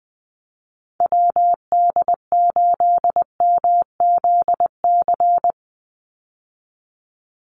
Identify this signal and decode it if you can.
Morse code, heard as a keyed tone, about 20 wpm, 700 Hz. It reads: WD8MZC
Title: USA Call Signs 20wpm